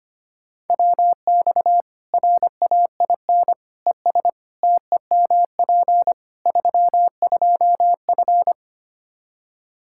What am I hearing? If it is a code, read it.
Morse code, heard as a keyed tone, about 25 wpm, 700 Hz. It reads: WX RAIN ES TEMP 32F
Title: K1USN SST QSO Practice 25wpm